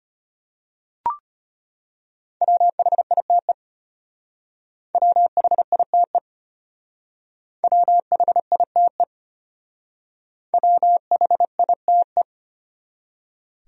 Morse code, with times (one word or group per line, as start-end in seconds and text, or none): 2.4-3.5 WHITE
4.9-6.2 WHITE
7.6-9.0 WHITE
10.5-12.2 WHITE